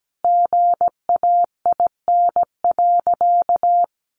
Morse code, heard as a keyed tone, about 17 wpm, 700 Hz. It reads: GAIN.
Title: Sentences from Top 1100 Words 17wpm